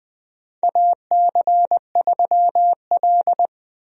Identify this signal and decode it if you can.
Morse code, heard as a keyed tone, about 20 wpm, 700 Hz. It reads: AC3L
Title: USA Call Signs - 4 Characters - Repeated 3x - 20wpm